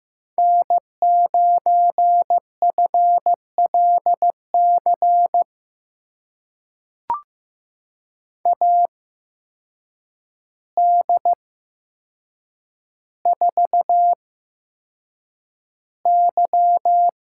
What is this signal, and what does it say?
Morse code, heard as a keyed tone, about 15 wpm, spaced as for 7 wpm, 700 Hz. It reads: N9FLC AD4Y
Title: USA Call Signs - 5 Characters - 8x Character Spacing - 15wpm